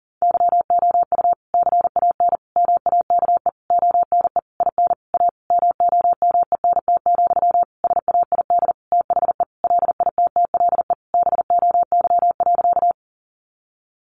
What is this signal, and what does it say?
Morse code, heard as a keyed tone, about 40 wpm, 700 Hz. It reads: YOU CAN MAKE ONE IN A MOMENT, SAID THE LITTLE BOY.
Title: Sentences from Top 1000 Words 40wpm